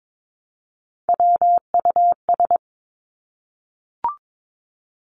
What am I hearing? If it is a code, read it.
Morse code, heard as a keyed tone, about 22 wpm, 700 Hz. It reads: WUS
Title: Three Challenging Letter-Number Combinations 22wpm